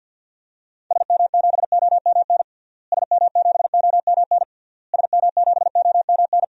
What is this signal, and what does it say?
Morse code, heard as a keyed tone, about 52 wpm, 700 Hz. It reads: SM7OMN SM7OMN SM7OMN
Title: DXCC-100 Call Signs - 6 Characters - Repeated 3x - 50wpm